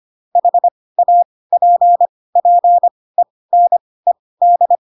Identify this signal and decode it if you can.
Morse code, heard as a keyed tone, about 25 wpm, spaced as for 18 wpm, 700 Hz. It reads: HAPPENED
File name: Pinocchio / Pinocchio ch1 12 25 ik1hge